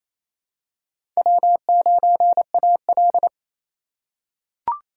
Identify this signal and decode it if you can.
Morse code, heard as a keyed tone, about 28 wpm, 700 Hz. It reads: W9AL
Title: USA Call Signs - 4x Character Spacing - 28wpm